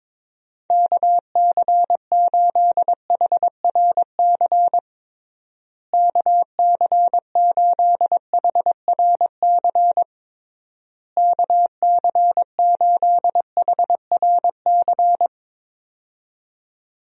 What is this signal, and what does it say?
Morse code, heard as a keyed tone, about 22 wpm, 700 Hz. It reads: KC8HRC KC8HRC KC8HRC
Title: USA Call Signs - 6 Characters - Repeated 3x - 22wpm